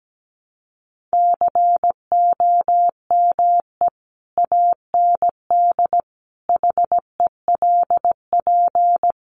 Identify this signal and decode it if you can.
Morse code, heard as a keyed tone, about 17 wpm, 700 Hz. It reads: COME AND HELP
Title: Sentences from Top 200 Words 17wpm